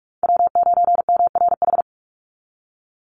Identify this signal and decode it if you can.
Morse code, heard as a keyed tone, about 44 wpm, 700 Hz. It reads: W9MRH
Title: NA SOTA Chaser Callsigns 45wpm